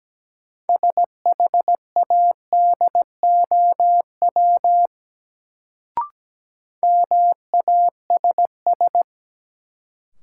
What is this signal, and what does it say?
Morse code, heard as a keyed tone, about 17 wpm, 700 Hz. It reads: SHADOW MASS
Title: Sets of 5 Words Encore - New Word Drill - 17wpm